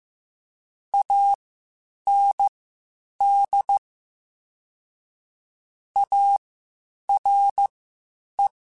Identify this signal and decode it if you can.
Morse code, heard as a keyed tone, about 15 wpm, spaced as for 8 wpm, 800 Hz. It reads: AND ARE
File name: (spaced) / QSOs Spaced-mp3s / WD0BGP 1 @13 sp